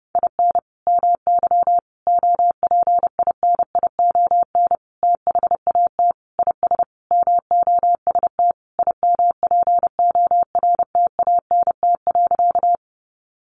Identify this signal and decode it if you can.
Morse code, heard as a keyed tone, about 30 wpm, 700 Hz. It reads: IN MY OPINION THAT IS MOST IMPORTANT.
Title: Sentences from Top 1200 Words 30wpm